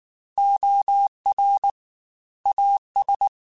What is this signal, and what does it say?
Morse code, heard as a keyed tone, about 19 wpm, 800 Hz. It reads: OR AS
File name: Technical Subjects / Technical Subjects-mp3s / Reactance @18